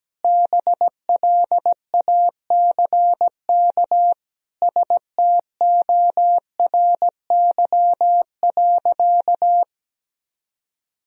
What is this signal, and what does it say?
Morse code, heard as a keyed tone, about 17 wpm, 700 Hz. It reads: BLACK STORY.
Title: Sentences from Top 1000 Words - Review Part 2 - 17wpm